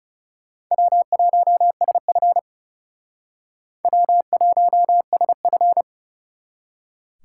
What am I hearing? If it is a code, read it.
Morse code, heard as a keyed tone, about 36 wpm, 700 Hz. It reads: W1SF W1SF
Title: Speed Racing - USA 1x2 Call Signs 30wpm